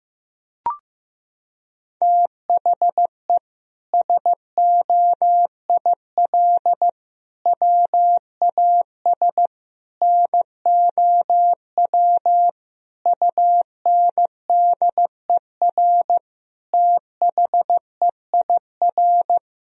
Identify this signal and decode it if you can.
Morse code, heard as a keyed tone, about 15 wpm, 700 Hz. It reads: THE SOIL WAS NOW UNDER THEIR
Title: Sentences from Top 1000 Words - Review Part 4 - 15wpm